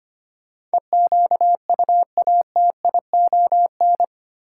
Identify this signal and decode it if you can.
Morse code, heard as a keyed tone, about 25 wpm, 700 Hz. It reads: EQUATION
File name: Eight Letter Words - 25wpm